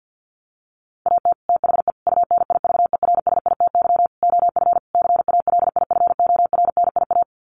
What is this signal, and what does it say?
Morse code, heard as a keyed tone, about 48 wpm, 700 Hz. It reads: AT THE UNIVERSITY OF CALIFORNIA
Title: Sets of 5 Words Encore2 50wpm